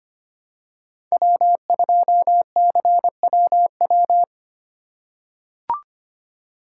Morse code, heard as a keyed tone, about 25 wpm, 700 Hz. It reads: W2CWW
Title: USA Call Signs - 5 Characters - 4x Character Spacing - 25wpm